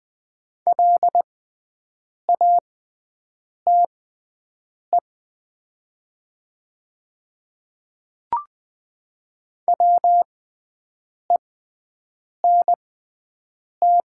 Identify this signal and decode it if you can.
Morse code, heard as a keyed tone, about 20 wpm, spaced as for 7 wpm, 700 Hz. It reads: LATE WENT
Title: 6x Character Spacing - Top 500 Words - 20wpm